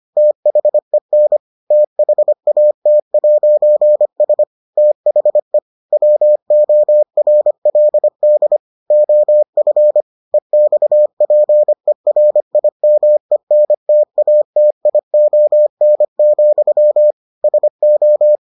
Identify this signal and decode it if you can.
Morse code, heard as a keyed tone, about 25 wpm, 600 Hz. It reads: THEN THAT'S THE WORLD OF EXPERIMENTATION, SO